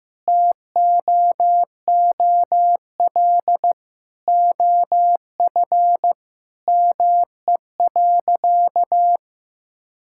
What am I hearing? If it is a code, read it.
Morse code, heard as a keyed tone, about 15 wpm, 700 Hz. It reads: TOOL OF ME.
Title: Sentences from Top 1000 Words - Review Part 6 - 15wpm